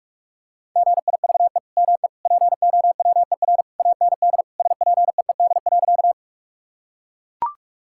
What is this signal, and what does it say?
Morse code, heard as a keyed tone, about 41 wpm, 700 Hz. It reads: GIVE ME POWER AND SPEED.
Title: Sentences from Top 1000 Words - Review Part 5 - 45wpm